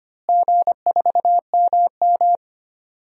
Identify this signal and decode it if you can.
Morse code, heard as a keyed tone, about 25 wpm, 700 Hz. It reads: G4MM
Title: DXCC-100 Call Signs - 4 Characters - Repeated 2x - 25wpm